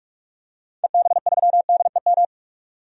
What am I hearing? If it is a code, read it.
Morse code, heard as a keyed tone, about 43 wpm, 700 Hz. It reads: EB3DEM
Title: DXCC-100 Call Signs - 6 Characters - 45wpm